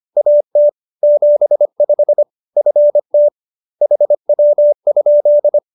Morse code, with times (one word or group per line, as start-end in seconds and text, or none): 0.2-0.7 AT
1.0-2.2 75
2.6-3.3 FT
3.8-5.6 HW?